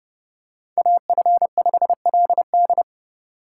Morse code, heard as a keyed tone, about 30 wpm, 700 Hz. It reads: AF5LD
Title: USA Call Signs - 3x Character Spacing - 30wpm